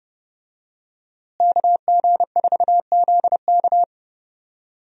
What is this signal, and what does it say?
Morse code, heard as a keyed tone, about 30 wpm, 700 Hz. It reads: KG4ZK
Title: USA Call Signs - 5 Characters - Repeated 3x - 30wpm